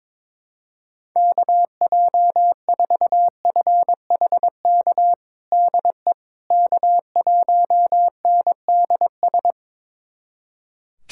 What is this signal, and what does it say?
Morse code, heard as a keyed tone, about 22 wpm, 700 Hz. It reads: KJ4FHK DE K1NDS
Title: USA Callsign DE Callsign 22wpm